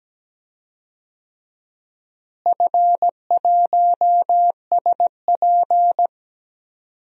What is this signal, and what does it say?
Morse code, heard as a keyed tone, about 17 wpm, 700 Hz. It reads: F1SP
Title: DXCC-100 Call Signs - 4 Characters - Repeated 2x - 17wpm